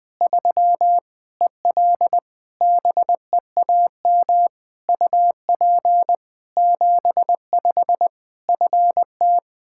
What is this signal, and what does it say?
Morse code, heard as a keyed tone, about 20 wpm, 700 Hz. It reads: 3 EL BEAM UP 75 FT
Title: K1USN SST QSO Practice 20wpm